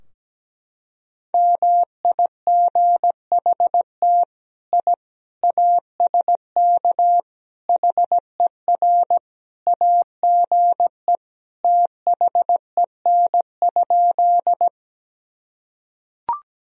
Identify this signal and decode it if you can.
Morse code, heard as a keyed tone, about 17 wpm, 700 Hz. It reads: MIGHT I ASK HER AGE THEN?
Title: Sentences from Top 1000 Words - Review Part 3 - 17wpm